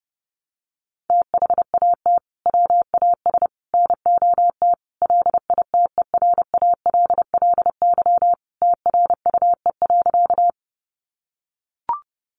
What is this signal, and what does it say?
Morse code, heard as a keyed tone, about 30 wpm, 700 Hz. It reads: THAT WAS NOT LITERALLY TRUE.
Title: Sentences from Top 1400 Words 30wpm